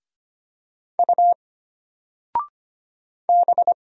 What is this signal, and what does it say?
Morse code, heard as a keyed tone, about 25 wpm, 700 Hz. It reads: U B
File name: Single-Letter-Number-25wpm